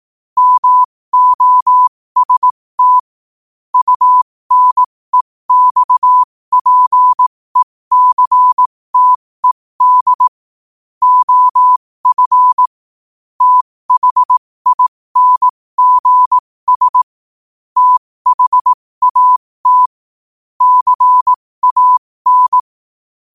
Morse code, heard as a keyed tone, about 18 wpm, spaced as for 15 wpm, 1000 Hz. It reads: MOST UNEXPECTED OF THINGS THAT CAN
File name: qotdfiles / 15wpm